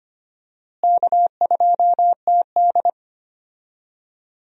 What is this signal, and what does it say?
Morse code, heard as a keyed tone, about 25 wpm, 700 Hz. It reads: K2TD
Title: USA Call Signs - 4 Characters - Repeated 2x - 25wpm